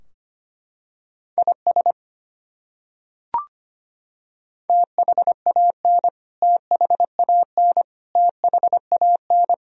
Morse code, heard as a keyed tone, about 25 wpm, 700 Hz. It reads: IS THAN THAN THAN
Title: Top 100 Words Repeated 25wpm